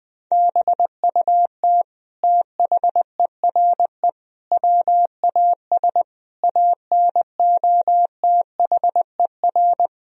Morse code, heard as a keyed tone, about 21 wpm, 700 Hz. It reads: BUT THERE WAS ANOTHER
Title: Sentences from Top 1000 Words - Review Part 2 - 20wpm